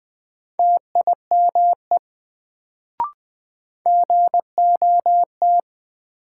Morse code, heard as a keyed tone, about 20 wpm, 700 Hz. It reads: TIME GOT
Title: Sets of 2 Words - New Word Drill - 20wpm